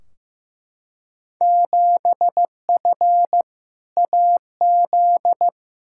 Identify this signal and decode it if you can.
Morse code, heard as a keyed tone, about 15 wpm, 700 Hz. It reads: 7F AZ
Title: Field Day Exchanges Encore - 15wpm